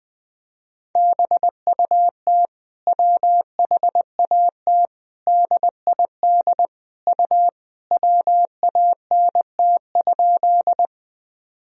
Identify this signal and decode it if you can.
Morse code, heard as a keyed tone, about 21 wpm, 700 Hz. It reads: BUT WHAT DID U WANT?
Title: Sentences from Top 200 Words and Abbreviations 20wpm